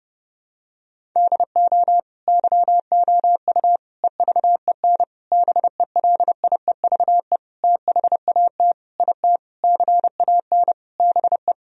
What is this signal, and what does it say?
Morse code, heard as a keyed tone, about 32 wpm, 700 Hz. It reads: DO YOU EVEN BELIEVE THAT IT CAN BE
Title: Sentences from Top 1000 Words - Review Part 1 - 30wpm